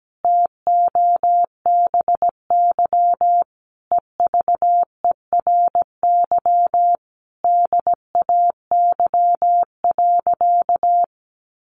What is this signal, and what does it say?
Morse code, heard as a keyed tone, about 17 wpm, 700 Hz. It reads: TOBY EVERY DAY.
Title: Easy Phrases 17wpm